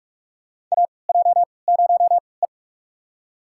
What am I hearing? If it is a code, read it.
Morse code, heard as a keyed tone, about 43 wpm, spaced as for 25 wpm, 700 Hz. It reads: AJ0E
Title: USA Call Signs - 3x Character Spacing - 45wpm